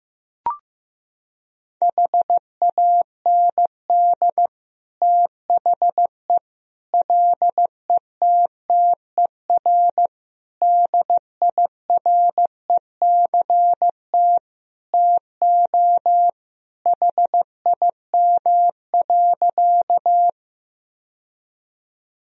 Morse code, heard as a keyed tone, about 15 wpm, 700 Hz. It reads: HAND THE LETTER DIRECT TO HIM.
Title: Sentences from Top 1000 Words - Review Part 2 - 15wpm